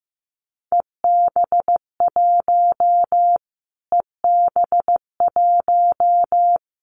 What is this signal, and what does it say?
Morse code, heard as a keyed tone, about 15 wpm, 700 Hz. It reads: EB1 EB1
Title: Callsign Prefixes for Top 25 DXCC - Repeated 3x - 15wpm